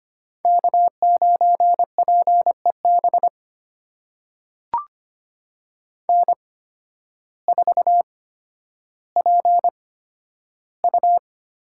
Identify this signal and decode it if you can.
Morse code, heard as a keyed tone, about 25 wpm, spaced as for 12 wpm, 700 Hz. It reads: K9PEB N4PU
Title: USA Call Signs - 8x Character Spacing - 25wpm